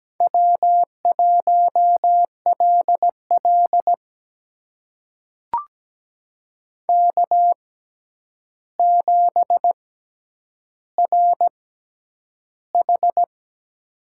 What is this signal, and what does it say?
Morse code, heard as a keyed tone, about 17 wpm, spaced as for 9 wpm, 700 Hz. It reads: W1LL K7RH
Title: USA Call Signs - 4 Characters - 6x Character Spacing - 17wpm